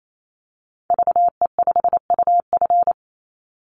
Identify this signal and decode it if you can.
Morse code, heard as a keyed tone, about 28 wpm, 700 Hz. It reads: VE5UF
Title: Call Signs Repeated 2x - 28wpm